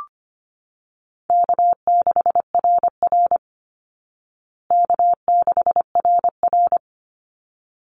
Morse code, heard as a keyed tone, about 25 wpm, 700 Hz. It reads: K6RR K6RR
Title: Easy Call Signs Repeated 2x - 25wpm